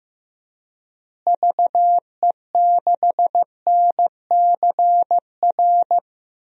Morse code, heard as a keyed tone, about 15 wpm, 700 Hz. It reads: VE6NCR
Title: DXCC-100 Call Signs - 6 Characters - Repeated 3x - 15wpm